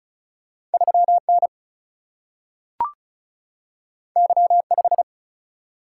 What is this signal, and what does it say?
Morse code, heard as a keyed tone, about 36 wpm, 700 Hz. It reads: 3N Y5
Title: Single-Letter Single-Number 35wpm